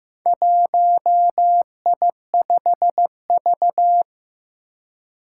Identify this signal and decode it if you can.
Morse code, heard as a keyed tone, about 15 wpm, 700 Hz. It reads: 1I5V
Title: Four Challenging Letter-Number Combinations 15wpm